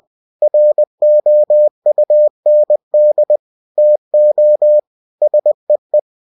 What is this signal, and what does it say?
Morse code, heard as a keyed tone, about 20 wpm, 600 Hz. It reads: ROUND TO SEE
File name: cw-books / winnie-the-pooh / winnie-20wpm-page-0034